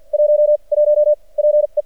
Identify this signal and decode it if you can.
Morse code, heard as a keyed tone, about 53 wpm, 600 Hz. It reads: 1 1 P E
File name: archived / CRSnet